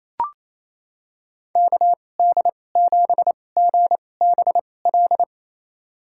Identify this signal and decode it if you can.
Morse code, heard as a keyed tone, about 28 wpm, spaced as for 20 wpm, 700 Hz. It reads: KD7GBL
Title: USA Call Signs - 6 Characters - 2x Character Spacing - 28wpm